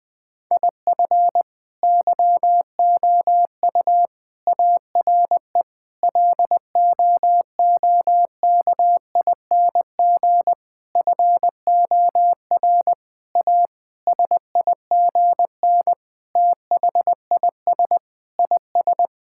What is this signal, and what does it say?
Morse code, heard as a keyed tone, about 21 wpm, 700 Hz. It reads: IF YOU ARE LOOKING FOR A SIGN THIS IS
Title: Sentences with 6-Word Phrases - 20wpm